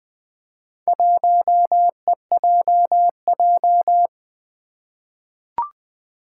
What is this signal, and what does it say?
Morse code, heard as a keyed tone, about 20 wpm, 700 Hz. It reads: 1EJJ
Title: Four Challenging Letter-Number Combinations 20wpm